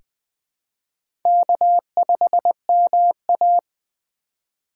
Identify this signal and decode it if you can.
Morse code, heard as a keyed tone, about 20 wpm, 700 Hz. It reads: K5MA
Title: K1USN SST QSO Practice - 20wpm